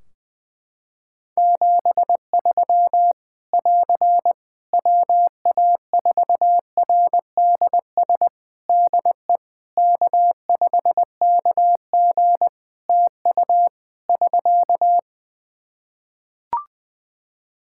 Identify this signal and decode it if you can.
Morse code, heard as a keyed tone, about 20 wpm, 700 Hz. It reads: 73 <AR> WA4RDS DE K5KG TU <SK>
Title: Practice QSO - Exchange 4 - Ending - 20wpm